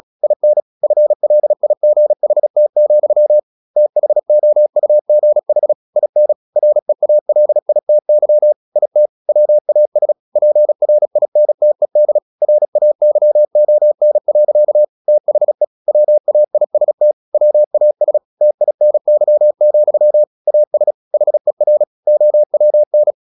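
Morse code, heard as a keyed tone, about 36 wpm, 600 Hz. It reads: IN FLIGHT, THOUGH IN REALITY IT WAS PRINTED RAYON. THE WAIST WAS TINY, AS HER OWN
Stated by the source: Morse practice at 36 wpm, text of Ken Follett's novel Never, from CW med Gnister.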